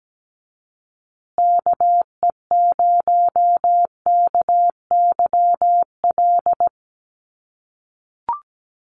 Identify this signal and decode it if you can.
Morse code, heard as a keyed tone, about 17 wpm, 700 Hz. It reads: KE0KYL
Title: USA Call Signs Repeated 3x 17wpm